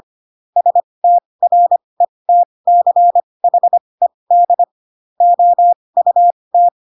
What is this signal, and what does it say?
Morse code, heard as a keyed tone, about 25 wpm, spaced as for 20 wpm, 700 Hz. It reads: STRETCHED OUT
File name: Pinocchio / Pinocchio ch7 15 25 ik1hge